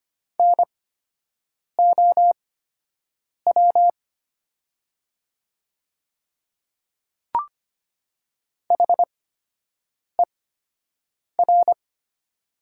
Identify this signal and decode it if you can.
Morse code, heard as a keyed tone, about 25 wpm, spaced as for 7 wpm, 700 Hz. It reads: NOW HER
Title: 8x Character Spacing - Top 100 Words - 25wpm